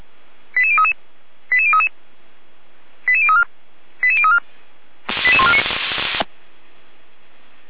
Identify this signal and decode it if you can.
Five-tone selective call (ZVEI), heard as quick runs of five tones, tones of about 70 ms. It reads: 88033 88033 88034 88034 88149